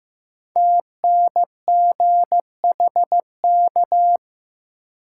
Morse code, heard as a keyed tone, about 15 wpm, 700 Hz. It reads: TNGHK
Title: Code Groups - 5 Letter:Numbers - 15wpm